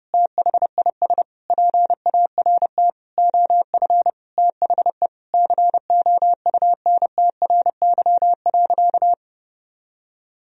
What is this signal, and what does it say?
Morse code, heard as a keyed tone, about 30 wpm, 700 Hz. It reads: THIS PART OF THE COUNTRY.
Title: Sentences from Top 300 Words and Abbreviations 30wpm